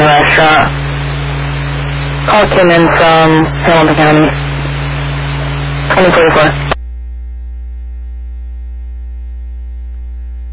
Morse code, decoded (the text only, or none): none